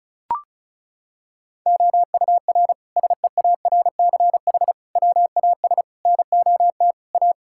Morse code, heard as a keyed tone, about 34 wpm, 700 Hz. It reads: OUR SEARCH WAS NOT A
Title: Sentences from Top 1000 Words - Review Part 5 - 35wpm